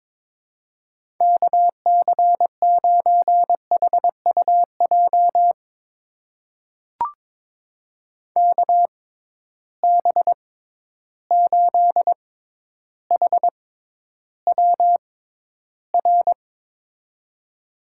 Morse code, heard as a keyed tone, about 22 wpm, spaced as for 11 wpm, 700 Hz. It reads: KC9HUJ KB8HWR
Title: USA Call Signs - 6 Characters - 6x Character Spacing - 22wpm